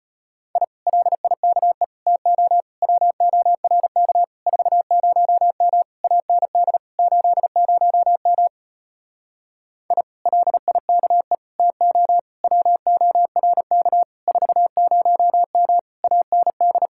Morse code, heard as a keyed tone, about 33 wpm, 700 Hz. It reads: I LIKE TO WORK 40M AND 80M I LIKE TO WORK 40M AND